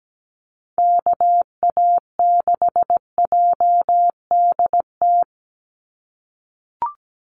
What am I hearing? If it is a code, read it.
Morse code, heard as a keyed tone, about 17 wpm, 700 Hz. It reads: KA6JDT
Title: USA Call Signs 17wpm